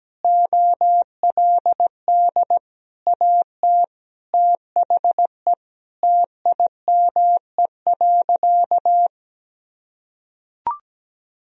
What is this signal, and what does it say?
Morse code, heard as a keyed tone, about 17 wpm, 700 Hz. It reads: OLD AT THE TIME.
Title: Sentences from Top 1000 Words - Review Part 2 - 17wpm